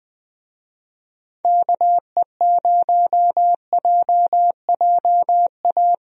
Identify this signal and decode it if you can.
Morse code, heard as a keyed tone, about 20 wpm, 700 Hz. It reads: KE0JJA
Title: USA Call Signs - 6 Characters - 20wpm